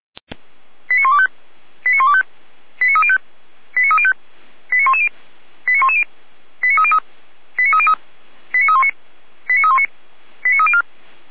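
Five-tone selective call (ZVEI), heard as quick runs of five tones, tones of about 70 ms. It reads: 89125 89125 89305 89305 89110 89110 89303 89303 89319 89319 89304
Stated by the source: ZVEI five-tone selective call 89265 received with an SDR.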